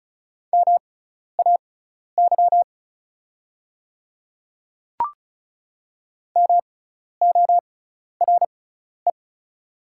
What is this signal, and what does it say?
Morse code, heard as a keyed tone, about 36 wpm, spaced as for 12 wpm, 690 Hz. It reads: MAY MORE
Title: 6x Character Spacing - Top 200 Words - 35wpm